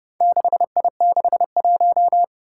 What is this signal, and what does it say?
Morse code, heard as a keyed tone, about 30 wpm, 700 Hz. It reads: 6I61
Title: Four Challenging Letter-Number Combinations 30wpm